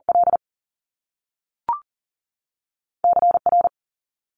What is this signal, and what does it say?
Morse code, heard as a keyed tone, about 41 wpm, spaced as for 6 wpm, 700 Hz. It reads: LCR
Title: Top Two Letter Combinations in Words 40wpm